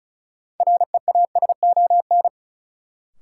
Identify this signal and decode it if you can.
Morse code, heard as a keyed tone, about 36 wpm, 700 Hz. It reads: REASON
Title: Sets of 5 Words Encore - New Word Drill - 35wpm